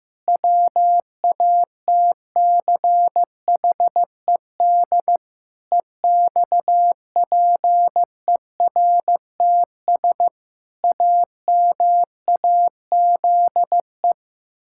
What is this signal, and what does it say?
Morse code, heard as a keyed tone, about 15 wpm, 700 Hz. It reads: WATCHED EXPERTS AMAZE
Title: Pangram Challenge v2 - 15wpm